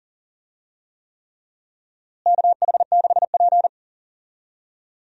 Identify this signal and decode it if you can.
Morse code, heard as a keyed tone, about 40 wpm, 700 Hz. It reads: KH6P
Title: DXCC-100 Call Signs - 4 Characters - Repeated 3x - 40wpm